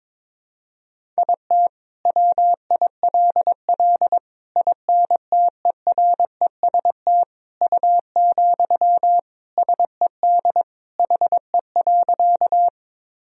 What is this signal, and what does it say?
Morse code, heard as a keyed tone, about 22 wpm, 700 Hz. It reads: IT WILL INTEREST U, SED HE.
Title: Sentences from Top 400 Words and Abbreviations 22wpm